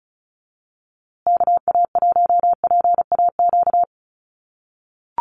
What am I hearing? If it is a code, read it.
Morse code, heard as a keyed tone, about 34 wpm, 700 Hz. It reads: KA1PAQ
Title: USA Call Signs Repeated 3x 35wpm